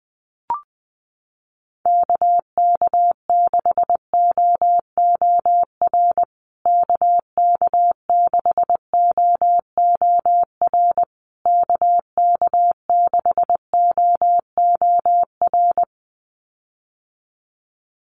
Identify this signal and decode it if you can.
Morse code, heard as a keyed tone, about 19 wpm, 700 Hz. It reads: KK6OOR KK6OOR KK6OOR
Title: USA Call Signs Repeated 3x 20wpm